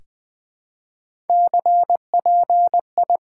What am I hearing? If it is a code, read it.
Morse code, heard as a keyed tone, about 20 wpm, 700 Hz. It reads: CPI
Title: Top 90 QSO Elements - 4x Character Spacing - 20wpm